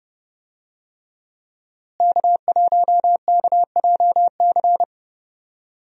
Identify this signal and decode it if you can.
Morse code, heard as a keyed tone, about 31 wpm, 700 Hz. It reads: K1KJC
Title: USA Call Signs - 5 Characters - Repeated 3x - 30wpm